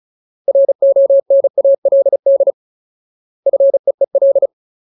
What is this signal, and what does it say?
Morse code, heard as a keyed tone, about 34 wpm, 550 Hz. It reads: RONALD FEEL
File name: Ultimate Word List / 35WPM Ultimate